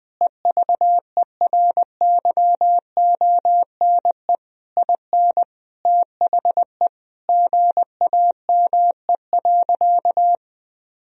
Morse code, heard as a keyed tone, about 20 wpm, 700 Hz. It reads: EVERYONE IN THE GAME.
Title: Sentences from Top 1000 Words - Review Part 4 - 20wpm